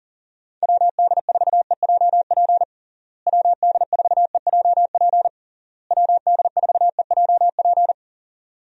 Morse code, heard as a keyed tone, about 40 wpm, 700 Hz. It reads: WD4EJP WD4EJP WD4EJP
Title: USA Call Signs - 6 Characters - Repeated 3x - 40wpm